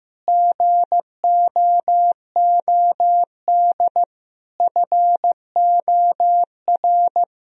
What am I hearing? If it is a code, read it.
Morse code, heard as a keyed tone, about 16 wpm, 700 Hz. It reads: GOOD FOR
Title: Sentences from Top 1000 Words - Review Part 3 - 15wpm